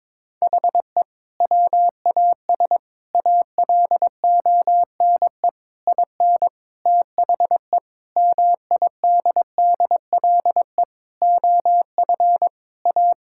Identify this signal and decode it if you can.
Morse code, heard as a keyed tone, about 22 wpm, 700 Hz. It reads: HE WAS ALONE IN THE MIDDLE OF A
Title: Sentences from Top 1000 Words - Review Part 6 - 22wpm